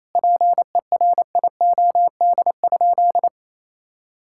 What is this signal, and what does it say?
Morse code, heard as a keyed tone, about 27 wpm, 700 Hz. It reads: PERIOD?
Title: Sentences from Top 1000 Words - Review Part 5 - 28wpm